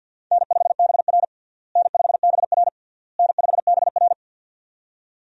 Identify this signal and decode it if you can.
Morse code, heard as a keyed tone, about 48 wpm, 700 Hz. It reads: N5BR N5BR N5BR
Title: USA Call Signs - 4 Characters - Repeated 3x - 50wpm